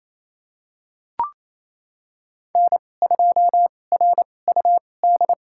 Morse code, heard as a keyed tone, about 28 wpm, spaced as for 20 wpm, 700 Hz. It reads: N2RUD
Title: USA Call Signs - 2x Character Spacing - 28wpm